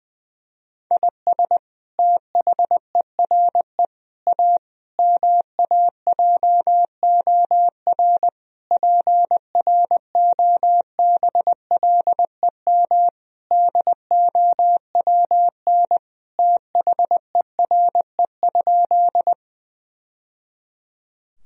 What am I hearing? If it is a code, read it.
Morse code, heard as a keyed tone, about 21 wpm, 700 Hz. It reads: IS THERE A MAJOR PROBLEM DOWN THERE?
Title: Sentences from Top 1000 Words - Review Part 4 - 20wpm